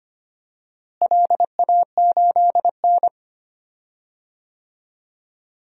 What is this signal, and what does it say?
Morse code, heard as a keyed tone, about 25 wpm, 700 Hz. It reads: LA8N
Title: DXCC-100 Call Signs - 4 Characters - Repeated 3x - 25wpm